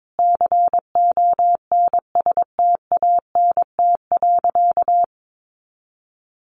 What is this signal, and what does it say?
Morse code, heard as a keyed tone, about 22 wpm, 700 Hz. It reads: CONSTANT.
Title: Sentences from Top 1500 Words 22wpm